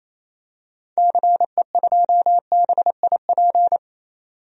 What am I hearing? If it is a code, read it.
Morse code, heard as a keyed tone, about 28 wpm, 700 Hz. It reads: CE2BIP
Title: DXCC-100 Call Signs - 6 Characters - 28wpm